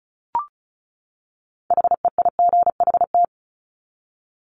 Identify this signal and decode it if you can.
Morse code, heard as a keyed tone, about 34 wpm, 700 Hz. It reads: HEIGHT
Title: Top 1501-1750 Words 35wpm